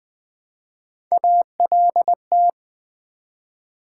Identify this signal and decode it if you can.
Morse code, heard as a keyed tone, about 21 wpm, 700 Hz. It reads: ALT
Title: Build A Word - 1 Letter At A Time - 3 to 5-Letter Words - 20wpm